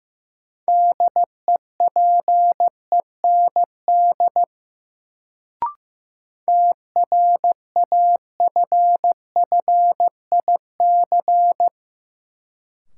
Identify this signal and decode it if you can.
Morse code, heard as a keyed tone, about 15 wpm, 700 Hz. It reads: DEPEND TRAFFIC
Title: Sentences from Top 1300 Words - New Word Drill - 15wpm